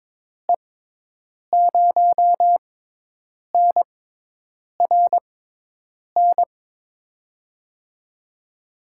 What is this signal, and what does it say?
Morse code, heard as a keyed tone, about 21 wpm, spaced as for 8 wpm, 700 Hz. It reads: E0NRN
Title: USA Call Signs - 6x Character Spacing - 22wpm